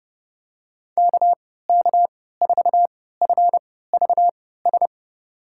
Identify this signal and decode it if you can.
Morse code, heard as a keyed tone, about 29 wpm, spaced as for 17 wpm, 700 Hz. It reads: KK4FVS
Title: USA Call Signs - 6 Characters - 3x Character Spacing - 30wpm